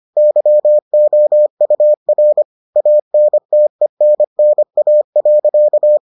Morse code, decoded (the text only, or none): YOUR ANTENNA.